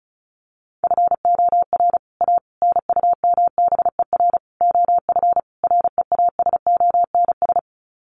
Morse code, heard as a keyed tone, about 34 wpm, 700 Hz. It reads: FOR A NUMBER OF REASONS
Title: Sets of 5 Words Encore2 35wpm